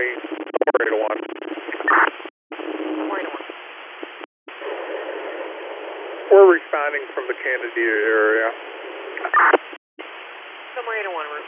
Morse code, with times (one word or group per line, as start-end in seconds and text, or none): none